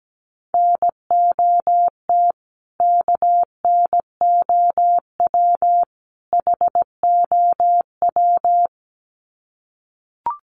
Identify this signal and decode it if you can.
Morse code, heard as a keyed tone, about 17 wpm, 700 Hz. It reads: NOT KNOW HOW
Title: Sets of 3 Words 17wpm